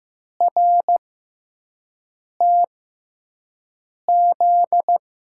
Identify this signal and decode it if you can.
Morse code, heard as a keyed tone, about 15 wpm, spaced as for 5 wpm, 700 Hz. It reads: RTZ
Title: USA Call Signs - 5 Characters - 6x Character Spacing - 15wpm